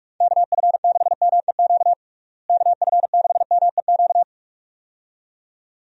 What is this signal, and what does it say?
Morse code, heard as a keyed tone, about 46 wpm, 700 Hz. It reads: KF6MEQ KF6MEQ
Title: USA Call Signs - 6 Characters - Repeated 2x - 45wpm